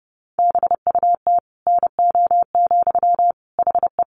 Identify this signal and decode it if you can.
Morse code, heard as a keyed tone, about 30 wpm, 700 Hz. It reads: BUT NO, HE
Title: Sentences from Top 1400 Words 30wpm